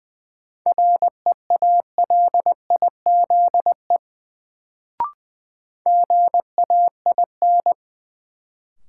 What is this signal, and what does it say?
Morse code, heard as a keyed tone, about 20 wpm, 700 Hz. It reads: REALIZE GAIN
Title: Sentences from Top 1000 Words - New Word Drill - 20wpm